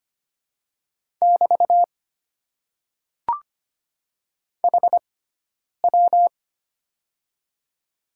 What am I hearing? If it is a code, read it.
Morse code, heard as a keyed tone, about 25 wpm, spaced as for 9 wpm, 700 Hz. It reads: <BT> HW
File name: Top 90 QSO Elements - 6x Character Spacing - 25wpm